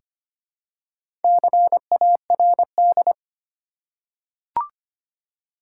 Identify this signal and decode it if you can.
Morse code, heard as a keyed tone, about 25 wpm, 690 Hz. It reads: CARD
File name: practice / 25wpm / Easy Words with Hints - 25wpm